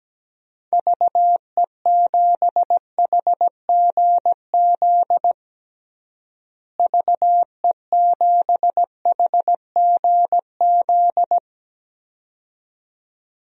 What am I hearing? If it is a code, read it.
Morse code, heard as a keyed tone, about 17 wpm, 700 Hz. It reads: VE7HGZ VE7HGZ
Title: DXCC-100 Call Signs - 6 Characters - Repeated 2x - 17wpm